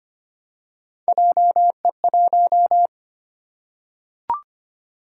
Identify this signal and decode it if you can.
Morse code, heard as a keyed tone, about 25 wpm, 700 Hz. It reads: JE1
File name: Three Letter-Number Combinations - 25wpm